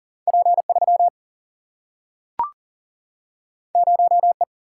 Morse code, heard as a keyed tone, about 40 wpm, 700 Hz. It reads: P3 0E
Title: Single-Letter Single-Number 40wpm